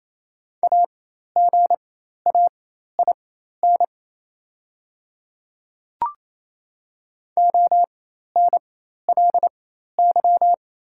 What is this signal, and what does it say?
Morse code, heard as a keyed tone, about 28 wpm, spaced as for 13 wpm, 690 Hz. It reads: AGAIN ONLY
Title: 4x Character Spacing - Top 200 Words - 28wpm